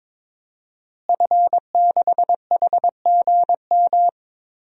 Morse code, heard as a keyed tone, about 22 wpm, 700 Hz. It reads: F6HGM
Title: DXCC-100 Call Signs - 5 Characters - Repeated 2x - 22wpm